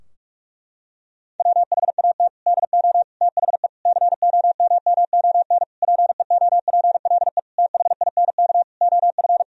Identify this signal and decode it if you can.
Morse code, heard as a keyed tone, about 42 wpm, 700 Hz. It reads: WHAT DO THE COMMON PEOPLE THINK OF
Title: Sentences from Top 1000 Words - Review Part 3 - 45wpm